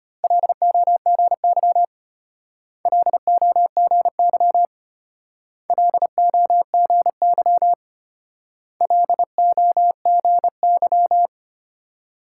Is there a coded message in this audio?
Yes: Morse code from 0.2 to 11.3 s, heard as a keyed tone, about 32 wpm, 700 Hz. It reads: LOGY LOGY LOGY LOGY